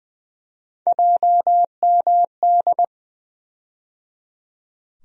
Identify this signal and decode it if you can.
Morse code, heard as a keyed tone, about 20 wpm, 700 Hz. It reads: JMD
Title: USA Call Sign Postfixes - 3 Letters 20wpm